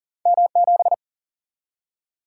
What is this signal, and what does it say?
Morse code, heard as a keyed tone, about 41 wpm, 700 Hz. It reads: M7
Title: Single-Letter Single-Number 40wpm